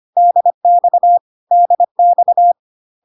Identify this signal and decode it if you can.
Morse code, heard as a keyed tone, about 25 wpm, 700 Hz. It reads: DX DX
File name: cw / CW common abbreviations alphabetical repeated 3 times 25 25wpm ik1hge